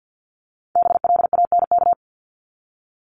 Morse code, heard as a keyed tone, about 50 wpm, 700 Hz. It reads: BLANK
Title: Top 1251-1500 Words 50wpm